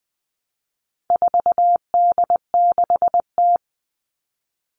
Morse code, heard as a keyed tone, about 20 wpm, 700 Hz. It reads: 4D6T
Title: All Cut Numbers - 4-Digit 20wpm